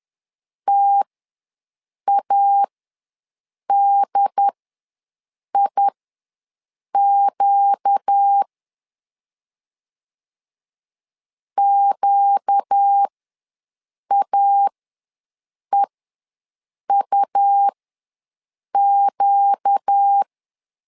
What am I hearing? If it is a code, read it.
Morse code, heard as a keyed tone, about 10 wpm, spaced as for 6 wpm, 800 Hz. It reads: TADIQ QAEUQ
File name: LEZIONI / 4 Q ws 10wpm 2306